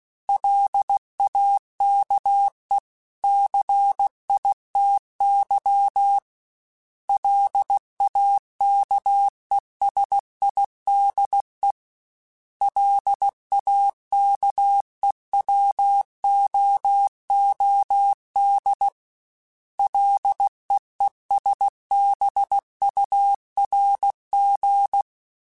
Morse code, heard as a keyed tone, about 16 wpm, 800 Hz. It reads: LAKE CITY LAKESIDE LAKEWOOD LEESBURG
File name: fromSMs / FromSMs mp3s / Cities@15